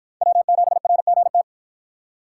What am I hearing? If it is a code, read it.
Morse code, heard as a keyed tone, about 49 wpm, 700 Hz. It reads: W7RGT